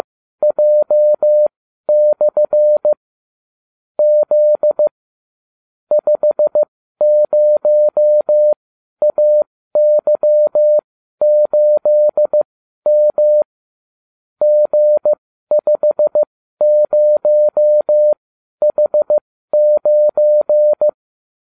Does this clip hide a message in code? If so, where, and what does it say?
Yes: Morse code from 0.4 to 20.9 s, heard as a keyed tone, about 15 wpm, spaced as for 12 wpm, 600 Hz. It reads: J/ Z 50AY8M G50H9